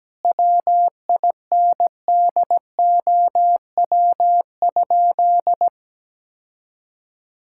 Morse code, heard as a keyed tone, about 17 wpm, 700 Hz. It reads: WINDOW?
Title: Sentences from Top 1600 Words 17wpm